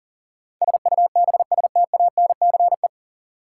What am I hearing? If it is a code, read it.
Morse code, heard as a keyed tone, about 40 wpm, 700 Hz. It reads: SUBSTANCE
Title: Sentences from Top 1500 Words - New Word Drill Alt - 40wpm